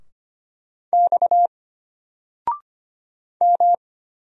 Morse code, heard as a keyed tone, about 25 wpm, 700 Hz. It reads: X M
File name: Single-Letter-Number-25wpm